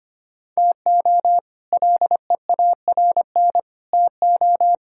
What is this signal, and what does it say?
Morse code, heard as a keyed tone, about 25 wpm, 700 Hz. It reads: TO LEARN TO
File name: practice / 25wpm / All-Age Appropriate Jokes - 25wpm